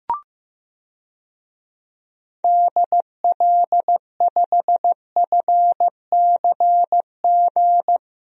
Morse code, heard as a keyed tone, about 15 wpm, 700 Hz. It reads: DL5FCG
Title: DXCC-100 Call Signs - 6 Characters - Repeated 2x - 15wpm